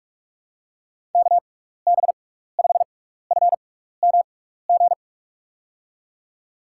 Morse code, heard as a keyed tone, about 46 wpm, spaced as for 16 wpm, 700 Hz. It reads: KB5FMG
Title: USA Call Signs - 6 Characters - 6x Character Spacing - 45wpm